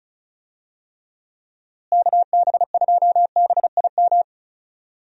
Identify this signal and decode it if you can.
Morse code, heard as a keyed tone, about 34 wpm, 700 Hz. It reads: KB2BIM
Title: USA Call Signs - 6 Characters - Repeated 2x - 35wpm